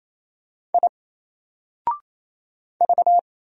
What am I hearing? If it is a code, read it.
Morse code, heard as a keyed tone, about 27 wpm, 700 Hz. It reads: I V
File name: Single-Letter-Number-28wpm